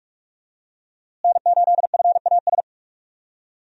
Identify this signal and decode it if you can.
Morse code, heard as a keyed tone, about 46 wpm, 700 Hz. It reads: N8FAS
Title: USA Call Signs - 5 Characters - Repeated 2x - 45wpm